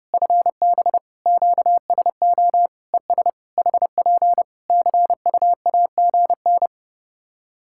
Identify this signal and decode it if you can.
Morse code, heard as a keyed tone, about 30 wpm, 700 Hz. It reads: FB QSO ES HP CUAGN
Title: Practice QSO - Exchange 4 - Ending - 30wpm